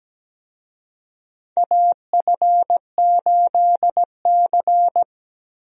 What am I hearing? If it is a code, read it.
Morse code, heard as a keyed tone, about 17 wpm, 700 Hz. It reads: AF8C
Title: USA Call Signs - 4 Characters - 17wpm